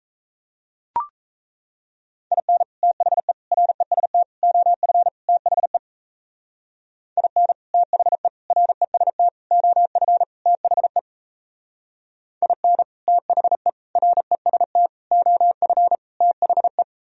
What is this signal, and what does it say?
Morse code, heard as a keyed tone, about 35 wpm, 700 Hz. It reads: IN THE REST OF THE IN THE REST OF THE IN THE REST OF THE
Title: Speed-Racing - Sets of 5 Words 28wpm